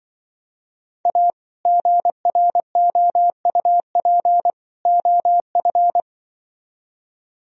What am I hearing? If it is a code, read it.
Morse code, heard as a keyed tone, about 23 wpm, 700 Hz. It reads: A GROUP OF
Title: Speed-Racing - Sets of 3 Words 20wpm